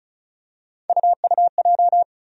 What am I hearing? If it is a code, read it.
Morse code, heard as a keyed tone, about 36 wpm, 700 Hz. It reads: UUJ
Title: USA Call Sign Postfixes - 3 Letters 35wpm